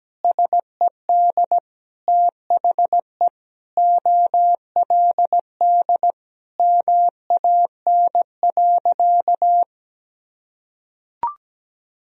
Morse code, heard as a keyed tone, about 17 wpm, 700 Hz. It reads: SED THE OLD MAN.
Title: Sentences from Top 200 Words and Abbreviations 17wpm